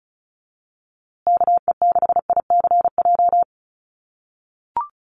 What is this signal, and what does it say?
Morse code, heard as a keyed tone, about 34 wpm, 700 Hz. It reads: KE6ICJ
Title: USA Call Signs Repeated 2x 35wpm